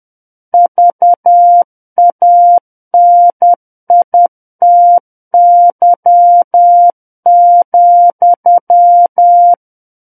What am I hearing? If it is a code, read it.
Morse code, heard as a keyed tone, about 10 wpm, 700 Hz. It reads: VANITY,
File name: Book One / 01-Chapter 1 - 10 WPM